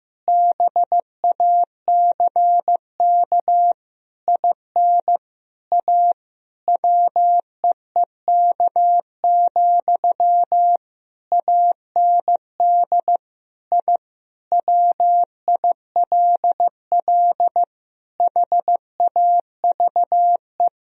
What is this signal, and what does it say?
Morse code, heard as a keyed tone, about 15 wpm, 700 Hz. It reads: BACK IN A WEEK, AND I WILL HAVE
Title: Sentences from Top 1000 Words - Review Part 3 - 15wpm